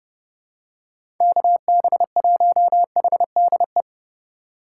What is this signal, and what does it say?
Morse code, heard as a keyed tone, about 32 wpm, 700 Hz. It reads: KB1HDE
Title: USA Call Signs - 6 Characters - 8x Character Spacing - 30wpm